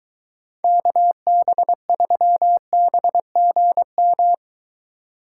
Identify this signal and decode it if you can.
Morse code, heard as a keyed tone, about 22 wpm, 700 Hz. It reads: KB3BGM